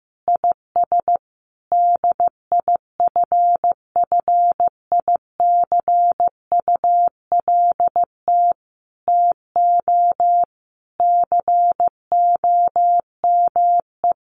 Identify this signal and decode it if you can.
Morse code, heard as a keyed tone, about 15 wpm, 700 Hz. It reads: IS DIFFICULT TO COME
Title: Sentences from Top 1500 Words 15wpm